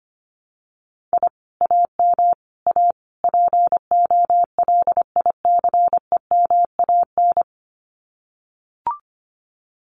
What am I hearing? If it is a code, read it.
Morse code, heard as a keyed tone, about 25 wpm, 700 Hz. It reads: I AM A POLICEMAN
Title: Common QSO Phrases Encore 25wpm